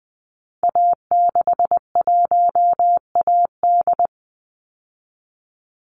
Morse code, heard as a keyed tone, about 20 wpm, 700 Hz. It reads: A61AD
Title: Call Signs Repeated 3x - 20wpm